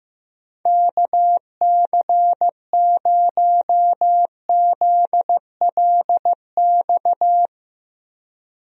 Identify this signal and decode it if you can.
Morse code, heard as a keyed tone, about 15 wpm, 700 Hz. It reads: KC0ZLX
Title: USA Call Signs - 6 Characters - Repeated 2x - 15wpm